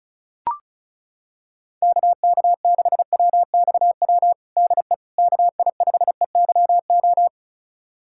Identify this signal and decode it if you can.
Morse code, heard as a keyed tone, about 36 wpm, 690 Hz. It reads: KK6WXW DE KI5EYO
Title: USA Callsign DE Callsign - 6 Characters - 35wpm